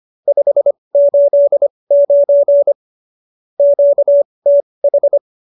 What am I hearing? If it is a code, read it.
Morse code, heard as a keyed tone, about 25 wpm, spaced as for 20 wpm, 570 Hz. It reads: 589 QTH